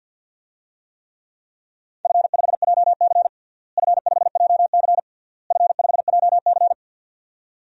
Morse code, heard as a keyed tone, about 52 wpm, 700 Hz. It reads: F5JC F5JC F5JC
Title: DXCC-100 Call Signs - 4 Characters - Repeated 3x - 50wpm